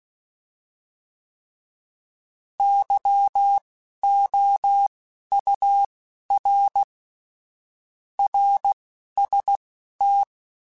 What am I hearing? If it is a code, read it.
Morse code, heard as a keyed tone, about 16 wpm, spaced as for 11 wpm, 800 Hz. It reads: YOUR RST